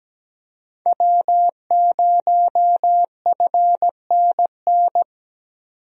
Morse code, heard as a keyed tone, about 17 wpm, 700 Hz. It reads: W0FNN
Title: USA Call Signs - 5 Characters - 17wpm